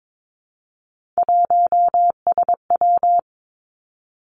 Morse code, heard as a keyed tone, about 22 wpm, 700 Hz. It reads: 1SW
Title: Three Challenging Letter-Number Combinations 22wpm